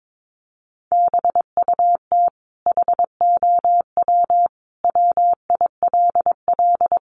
Morse code, heard as a keyed tone, about 22 wpm, 700 Hz. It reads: BUT HOW WILL
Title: Sentences from Top 1400 Words 22wpm